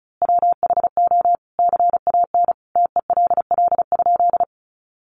Morse code, heard as a keyed tone, about 34 wpm, 700 Hz. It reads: WHO CAN TELL?
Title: A Sound Of Thunder 35wpm